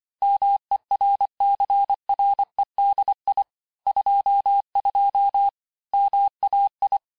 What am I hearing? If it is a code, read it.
Morse code, heard as a keyed tone, about 25 wpm, 800 Hz. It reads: MERCREDI 22 MAI
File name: texte-chiffre-vitesse-23mpm